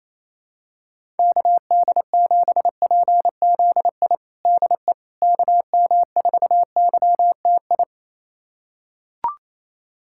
Morse code, heard as a keyed tone, about 28 wpm, 700 Hz. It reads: KD7PZI DE KM4YTI
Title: USA Callsign DE Callsign - 6 Characters - 28wpm